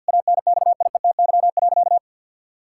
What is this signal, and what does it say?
Morse code, heard as a keyed tone, about 50 wpm, 700 Hz. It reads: ANXIETY.
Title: Sentences from Top 1600 Words 50wpm